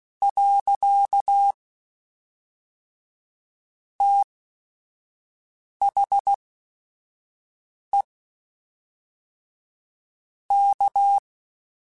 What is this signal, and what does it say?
Morse code, heard as a keyed tone, about 16 wpm, spaced as for 5 wpm, 800 Hz. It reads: . THE K
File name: Technical Subjects / Technical Subjects-mp3s / Reactance @5